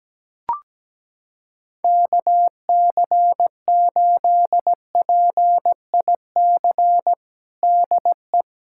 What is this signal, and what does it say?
Morse code, heard as a keyed tone, about 17 wpm, 700 Hz. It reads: KC8PIC DE
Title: USA Callsign DE Callsign - 6 Characters - 17wpm